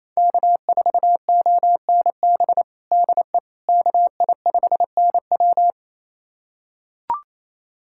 Morse code, heard as a keyed tone, about 28 wpm, 700 Hz. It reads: K4ONB DE KI5NW
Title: USA Callsign DE Callsign 28wpm